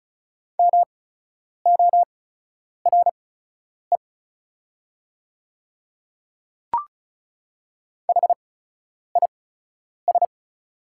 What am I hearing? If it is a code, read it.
Morse code, heard as a keyed tone, about 34 wpm, spaced as for 10 wpm, 690 Hz. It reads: MORE HIS
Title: 8x Character Spacing - Top 100 Words - 35wpm